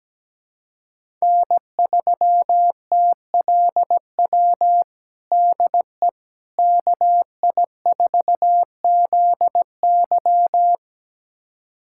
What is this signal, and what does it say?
Morse code, heard as a keyed tone, about 17 wpm, 700 Hz. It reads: N3TLW DE KI4ZY
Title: USA Callsign DE Callsign - 5 Characters - 17wpm